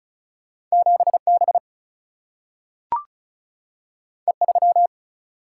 Morse code, heard as a keyed tone, about 36 wpm, 700 Hz. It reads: 7B E3
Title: Single-Letter Single-Number 35wpm